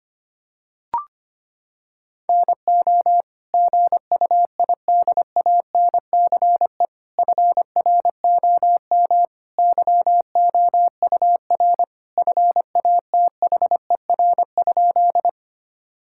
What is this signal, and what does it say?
Morse code, heard as a keyed tone, about 25 wpm, 700 Hz. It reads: NO GUIDANCE FROM YOUR FATHER?
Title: Sentences from Top 1600 Words 25wpm